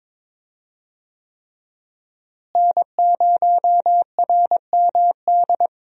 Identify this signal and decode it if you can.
Morse code, heard as a keyed tone, about 22 wpm, 700 Hz. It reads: N0RMD
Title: USA Call Signs - 5 Characters - Repeated 3x - 22wpm